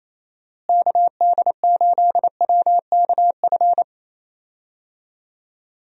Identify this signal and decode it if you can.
Morse code, heard as a keyed tone, about 28 wpm, 700 Hz. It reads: KD8WKF
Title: USA Call Signs - 6 Characters - Repeated 3x - 28wpm